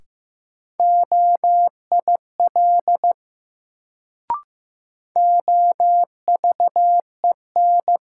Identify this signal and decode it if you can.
Morse code, heard as a keyed tone, about 15 wpm, 700 Hz. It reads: OIL OVEN
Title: Sentences from Top 1000 Words - Review Part 4 - New Word Drill - 15wpm